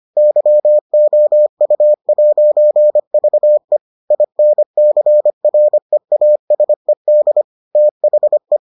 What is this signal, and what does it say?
Morse code, heard as a keyed tone, about 25 wpm, 600 Hz. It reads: YOU'VE INCREASED THE